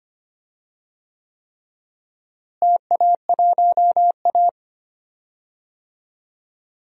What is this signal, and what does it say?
Morse code, heard as a keyed tone, about 25 wpm, 700 Hz. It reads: TA1A
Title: DXCC-100 Call Signs - 4 Characters - Repeated 3x - 25wpm